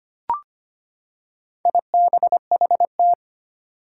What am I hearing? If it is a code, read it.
Morse code, heard as a keyed tone, about 25 wpm, 700 Hz. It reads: IBHT
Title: Four Challenging Letter-Number Combinations 25wpm